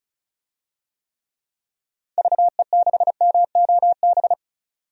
Morse code, heard as a keyed tone, about 36 wpm, 700 Hz. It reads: VE6MOB
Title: DXCC-100 Call Signs - 6 Characters - 35wpm